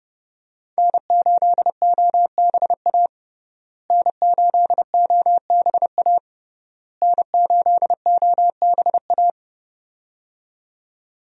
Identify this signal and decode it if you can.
Morse code, heard as a keyed tone, about 30 wpm, 700 Hz. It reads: N8OBA N8OBA N8OBA
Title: USA Call Signs - 5 Characters - Repeated 3x - 30wpm